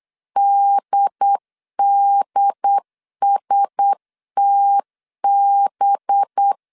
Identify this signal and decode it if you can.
Morse code, heard as a keyed tone, about 8 wpm, 800 Hz. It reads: DDSTB
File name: LEZIONI / 10 SIGNS / 10A SLASH 8wpm 3452